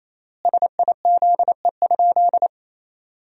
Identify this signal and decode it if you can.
Morse code, heard as a keyed tone, about 27 wpm, 700 Hz. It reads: SIZE?
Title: 3x Word-Spacing - Sentences from Top 500 Words - 28wpm